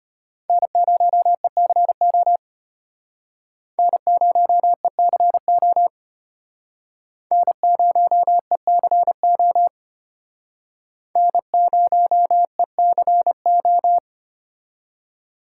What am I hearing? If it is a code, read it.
Morse code, heard as a keyed tone, about 34 wpm, 700 Hz. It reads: N0ECO N0ECO N0ECO N0ECO